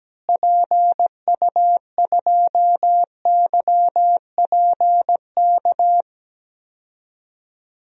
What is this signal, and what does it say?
Morse code, heard as a keyed tone, about 17 wpm, 700 Hz. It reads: PU2YPK
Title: DXCC-100 Call Signs - 6 Characters - Repeated 2x - 17wpm